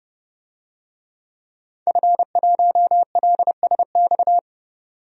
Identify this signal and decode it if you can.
Morse code, heard as a keyed tone, about 32 wpm, 700 Hz. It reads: F1LSX
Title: DXCC-100 Call Signs - 5 Characters - 30wpm